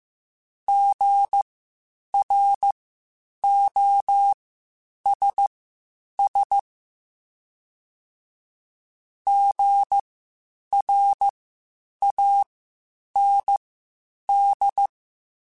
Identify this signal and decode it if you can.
Morse code, heard as a keyed tone, about 15 wpm, spaced as for 8 wpm, 800 Hz. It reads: GROSS GRAND
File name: Spaced Alphas / Spaced Alphas mp3s / Gs sp @13